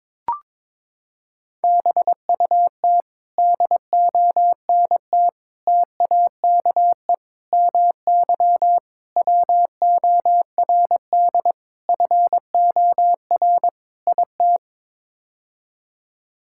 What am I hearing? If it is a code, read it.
Morse code, heard as a keyed tone, about 22 wpm, 700 Hz. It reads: BUT DONT TAKE MY WORD FOR IT
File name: practice / 22wpm / Sets of 7 Words - 22wpm